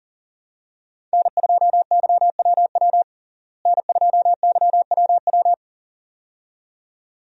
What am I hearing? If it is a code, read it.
Morse code, heard as a keyed tone, about 40 wpm, 700 Hz. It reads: N2YWW N2YWW
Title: USA Call Signs - 5 Characters - Repeated 2x - 40wpm